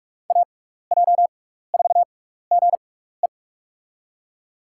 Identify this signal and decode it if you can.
Morse code, heard as a keyed tone, about 44 wpm, spaced as for 15 wpm, 690 Hz. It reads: AJ4GE
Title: USA Call Signs - 5 Characters - 6x Character Spacing - 45wpm